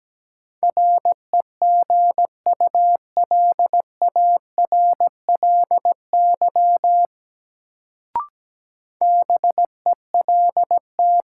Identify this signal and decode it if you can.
Morse code, heard as a keyed tone, about 17 wpm, 700 Hz. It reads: REGULARLY BELT
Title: Sentences from Top 1400 Words - New Word Drill Alt - 17wpm